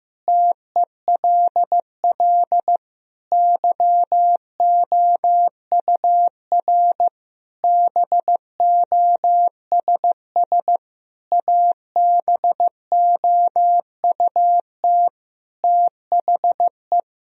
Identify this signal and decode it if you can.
Morse code, heard as a keyed tone, about 15 wpm, 700 Hz. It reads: TELL YOUR BOSS ABOUT THE
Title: Sentences from Top 1000 Words - Review Part 4 - 15wpm